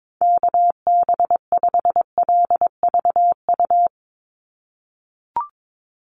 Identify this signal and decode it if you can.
Morse code, heard as a keyed tone, about 22 wpm, 700 Hz. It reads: KB5LVU
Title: USA Call Signs 22wpm